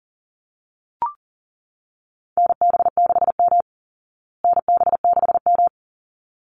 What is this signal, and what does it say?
Morse code, heard as a keyed tone, about 40 wpm, 700 Hz. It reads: NB6M NB6M
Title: Call Signs Repeated 3x - 40wpm